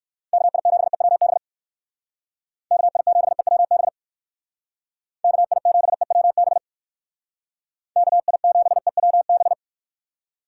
Morse code, heard as a keyed tone, about 58 wpm, 700 Hz. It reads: KI7EWB KI7EWB KI7EWB KI7EWB